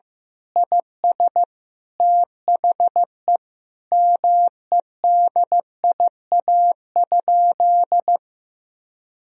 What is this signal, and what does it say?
Morse code, heard as a keyed tone, about 15 wpm, 700 Hz. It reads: IS THE MEDIA?
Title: Sentences from Top 1000 Words - Review Part 4 - 15wpm